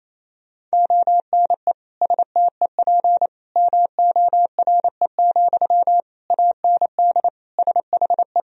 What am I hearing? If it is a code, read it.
Morse code, heard as a keyed tone, about 27 wpm, 690 Hz. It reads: ONE STEP MORE, AND SHE
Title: Sentences from Top 1000 Words - Review Part 3 - 28wpm